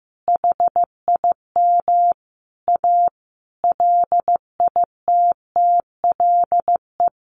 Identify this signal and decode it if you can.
Morse code, heard as a keyed tone, about 15 wpm, 700 Hz. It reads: HIM A LITTLE
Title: Sentences from Top 1500 Words 15wpm